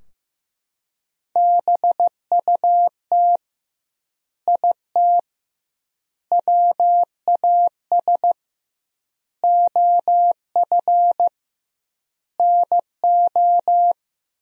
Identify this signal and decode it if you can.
Morse code, heard as a keyed tone, about 15 wpm, 700 Hz. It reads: BUT IT WAS OF NO
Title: 2x Word-Spacing - Sentences from Top 100 Words - 15wpm